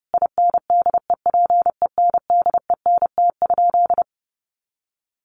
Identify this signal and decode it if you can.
Morse code, heard as a keyed tone, about 30 wpm, 700 Hz. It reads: INDEPENDENT?
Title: Sentences from Top 1400 Words 30wpm